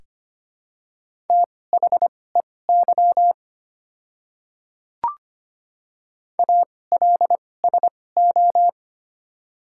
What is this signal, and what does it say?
Morse code, heard as a keyed tone, about 25 wpm, spaced as for 18 wpm, 700 Hz. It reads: THEY ALSO
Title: 2x Character Spacing - Top 500 Words - 25wpm